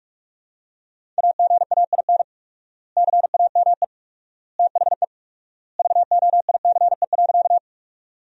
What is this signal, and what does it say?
Morse code, heard as a keyed tone, about 43 wpm, 700 Hz. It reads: AGAIN CAME THE VOICE.
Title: 4x Word-Spacing - Sentences from Top 500 Words - 45wpm